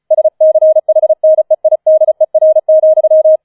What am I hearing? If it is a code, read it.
Morse code, heard as a keyed tone, about 36 wpm, 630 Hz. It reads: SCHNEIDER,